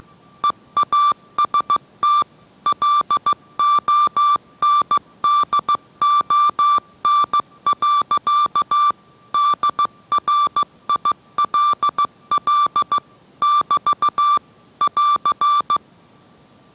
Morse code, heard as a keyed tone, about 15 wpm, 1170 Hz. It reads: EAST LONDON. DRILL <BT> <AR>